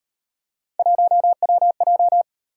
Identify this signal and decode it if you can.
Morse code, heard as a keyed tone, about 38 wpm, 700 Hz. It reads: 1WJ